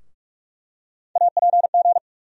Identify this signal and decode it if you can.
Morse code, heard as a keyed tone, about 47 wpm, 700 Hz. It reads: APG
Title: USA Call Sign Postfixes - 3 Letters 45wpm